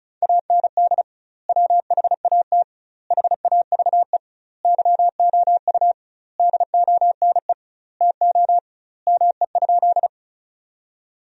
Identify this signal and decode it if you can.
Morse code, heard as a keyed tone, about 36 wpm, 700 Hz. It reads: AND WHAT HAVE YOU DONE TO ME?
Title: 2x Word-Spacing - Sentences from Top 500 Words - 35wpm